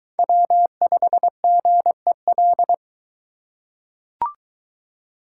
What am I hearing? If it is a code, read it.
Morse code, heard as a keyed tone, about 22 wpm, 700 Hz. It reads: W5GEL